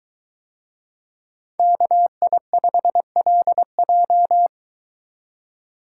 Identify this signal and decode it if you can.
Morse code, heard as a keyed tone, about 22 wpm, 700 Hz. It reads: KI5LJ